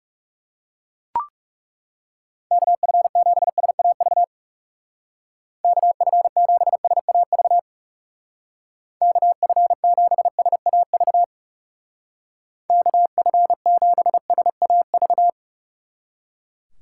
Morse code, heard as a keyed tone, about 38 wpm, 700 Hz. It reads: KF7SAV KF7SAV KF7SAV KF7SAV
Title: Speed Racing - USA 2x3 Call Signs 30wpm